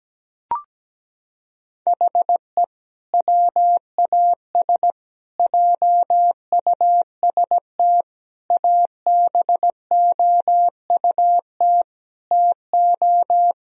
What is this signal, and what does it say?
Morse code, heard as a keyed tone, about 17 wpm, 700 Hz. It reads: HE WAS JUST ABOUT TO
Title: Sentences from Top 1000 Words - Review Part 2 - 17wpm